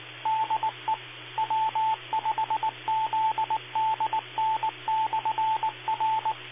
Morse code, heard as a keyed tone, about 20 wpm, 880 Hz, with radio noise. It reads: DE W5ZDN/R
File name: Amateur Radio Repeaters / OTARC FM TO 1